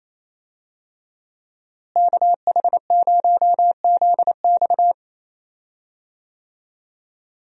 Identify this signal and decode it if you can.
Morse code, heard as a keyed tone, about 28 wpm, 700 Hz. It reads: KH0ZX
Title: USA Call Signs - 5 Characters - Repeated 3x - 28wpm